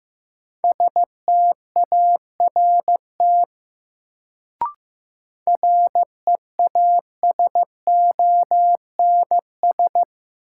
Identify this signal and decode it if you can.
Morse code, heard as a keyed tone, about 15 wpm, 700 Hz. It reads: START REASONS
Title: Sets of 4 Words - New Word Drill - 15wpm